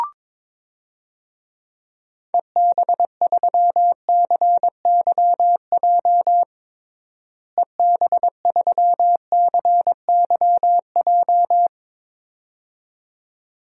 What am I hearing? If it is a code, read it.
Morse code, heard as a keyed tone, about 22 wpm, 700 Hz. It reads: EB3CYJ EB3CYJ
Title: DXCC-100 Call Signs - 6 Characters - Repeated 2x - 22wpm